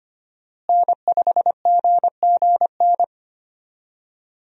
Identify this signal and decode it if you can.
Morse code, heard as a keyed tone, about 25 wpm, 700 Hz. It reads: N5GGN
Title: USA Call Signs - 5 Characters - Repeated 3x - 25wpm